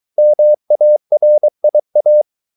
Morse code, heard as a keyed tone, about 23 wpm, 600 Hz. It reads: MARIA